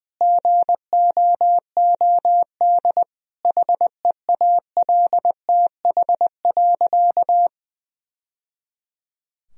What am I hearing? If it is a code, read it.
Morse code, heard as a keyed tone, about 20 wpm, 700 Hz. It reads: GOOD HEALTH.
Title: Sentences from Top 1000 Words - Review Part 3 - 20wpm